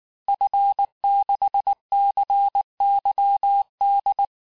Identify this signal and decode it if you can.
Morse code, heard as a keyed tone, about 19 wpm, 800 Hz. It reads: F6CYD
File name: cours-du-lundi files / cours-1583-fin-QSO-18-mpm